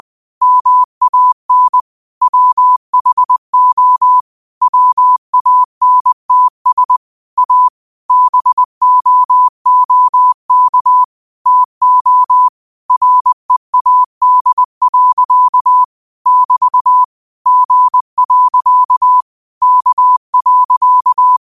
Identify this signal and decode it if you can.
Morse code, heard as a keyed tone, about 20 wpm, 1000 Hz. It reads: MAN WHO WANTS A BOOK TO READ. <BT> G. K.
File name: qotdfiles / 20wpm